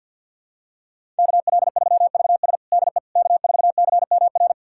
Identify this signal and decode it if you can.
Morse code, heard as a keyed tone, about 52 wpm, 700 Hz. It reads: KL3VS DE K4CMR
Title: USA Callsign DE Callsign - 5 Characters - 50wpm